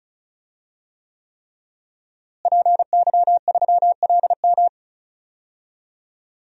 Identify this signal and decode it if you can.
Morse code, heard as a keyed tone, about 34 wpm, 700 Hz. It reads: PY3LM
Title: DXCC-100 Call Signs - 5 Characters - Repeated 3x - 35wpm